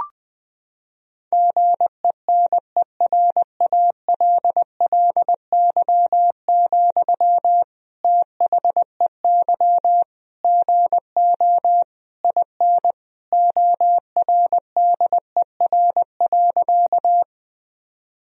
Morse code, heard as a keyed tone, about 21 wpm, 700 Hz. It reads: GENERALLY, THEY GO IN ORDER.
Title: Sentences from Top 1000 Words - Review Part 4 - 20wpm